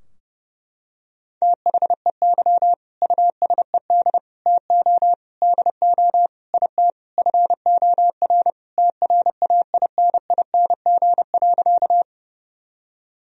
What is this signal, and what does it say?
Morse code, heard as a keyed tone, about 32 wpm, 700 Hz. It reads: THEY USED TO DO IT FOR TRAINING.
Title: Sentences from Top 1000 Words - Review Part 4 - 30wpm